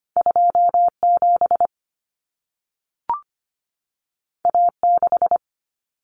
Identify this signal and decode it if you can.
Morse code, heard as a keyed tone, about 25 wpm, 700 Hz. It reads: 27 A6
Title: Common Cut Numbers - 2-Digit 25wpm